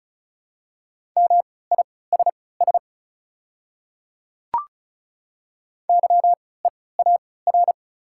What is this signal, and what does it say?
Morse code, heard as a keyed tone, about 36 wpm, spaced as for 20 wpm, 700 Hz. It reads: MISS YEAR
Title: 3x Character Spacing - Top 500 Words - 35wpm